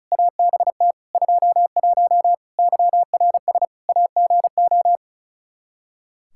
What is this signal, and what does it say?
Morse code, heard as a keyed tone, about 34 wpm, 700 Hz. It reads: ABT 21 YRS AGO
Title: Practice QSO - Exchange 3 - More Info - 35wpm